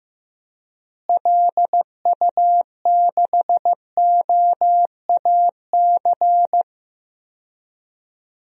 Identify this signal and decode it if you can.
Morse code, heard as a keyed tone, about 15 wpm, 700 Hz. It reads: LU6OAC
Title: DXCC-100 Call Signs - 6 Characters - Repeated 2x - 15wpm